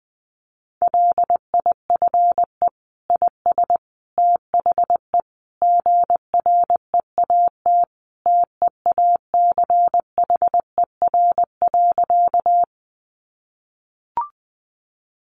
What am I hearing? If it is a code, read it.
Morse code, heard as a keyed tone, about 20 wpm, 700 Hz. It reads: LIFE IS THE GREAT TEACHER.
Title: Sentences from Top 900 Words 20wpm